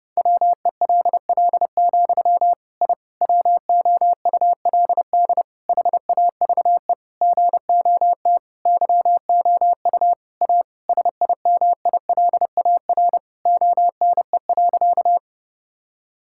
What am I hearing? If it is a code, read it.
Morse code, heard as a keyed tone, about 32 wpm, 700 Hz. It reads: WELL, I WOULD HAVE GOT YOU A SIMILAR ONE.
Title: Sentences from Top 1000 Words - Review Part 5 - 30wpm